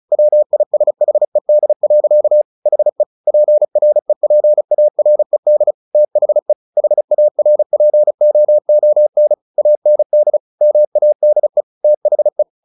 Morse code, heard as a keyed tone, about 36 wpm, 600 Hz. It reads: WISHED. HE PREPARED THE HARPOON AND MADE THE
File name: the old man and the sea 2 35wpm